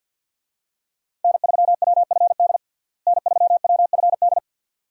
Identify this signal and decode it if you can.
Morse code, heard as a keyed tone, about 52 wpm, 700 Hz. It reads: N3WFD N3WFD
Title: USA Call Signs - 5 Characters - Repeated 2x - 50wpm